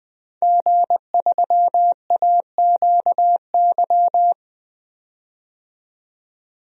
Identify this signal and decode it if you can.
Morse code, heard as a keyed tone, about 20 wpm, 700 Hz. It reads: G3AQY
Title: DXCC-100 Call Signs - 5 Characters - 20wpm